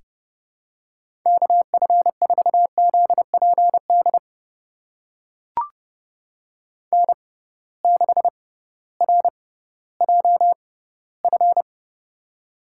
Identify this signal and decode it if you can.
Morse code, heard as a keyed tone, about 32 wpm, spaced as for 17 wpm, 700 Hz. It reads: KF4ZPD N6RJF
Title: USA Call Signs - 6x Character Spacing - 30wpm